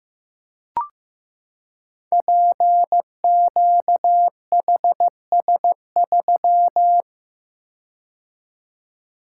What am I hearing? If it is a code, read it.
Morse code, heard as a keyed tone, about 15 wpm, 700 Hz. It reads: PQHS3
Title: Code Groups - 5 Letter:Numbers - 15wpm